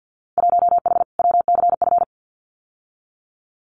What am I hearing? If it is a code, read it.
Morse code, heard as a keyed tone, about 50 wpm, 700 Hz. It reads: 1H WCF
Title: Winter Field Day Exchanges 50wpm